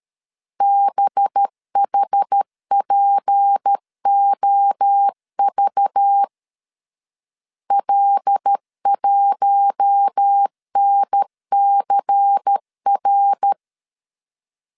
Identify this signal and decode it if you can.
Morse code, heard as a keyed tone, about 13 wpm, 800 Hz. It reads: BHPOV L1NCR
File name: LEZIONI / 6 PKX0 / 6 PKX0 12wpm 2231